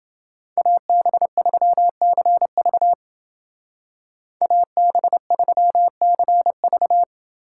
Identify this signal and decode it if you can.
Morse code, heard as a keyed tone, about 28 wpm, 700 Hz. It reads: AB3CV AB3CV